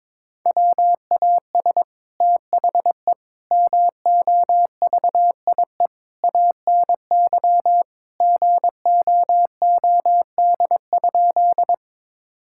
Morse code, heard as a keyed tone, about 23 wpm, 700 Hz. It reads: WAS THE MOVIE ANY GOOD?
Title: Sentences from Top 1000 Words - Review Part 5 - 22wpm